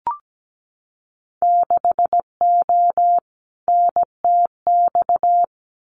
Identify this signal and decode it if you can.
Morse code, heard as a keyed tone, about 17 wpm, 700 Hz. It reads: 6O NTX
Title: Winter Field Day Exchanges Encore 17wpm